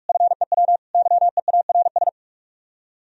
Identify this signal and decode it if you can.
Morse code, heard as a keyed tone, about 46 wpm, 700 Hz. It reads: FEW YEARS